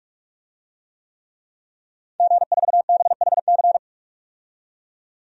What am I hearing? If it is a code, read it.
Morse code, heard as a keyed tone, about 48 wpm, 700 Hz. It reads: G4BHC
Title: DXCC-100 Call Signs - 5 Characters - 45wpm